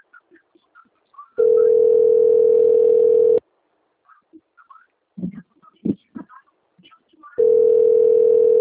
Ringback tone (North America), 440 + 480 Hz.
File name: out- - - - - .33167